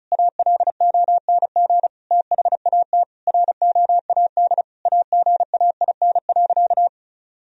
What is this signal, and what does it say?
Morse code, heard as a keyed tone, about 36 wpm, 700 Hz. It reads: ALONG THAT ROAD AGAIN.
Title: Sentences from Top 1000 Words - Review Part 2 - 35wpm